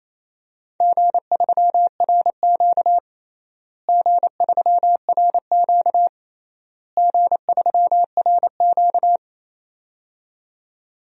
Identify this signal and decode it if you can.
Morse code, heard as a keyed tone, about 27 wpm, 700 Hz. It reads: G3RQ G3RQ G3RQ
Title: DXCC-100 Call Signs - 4 Characters - Repeated 3x - 28wpm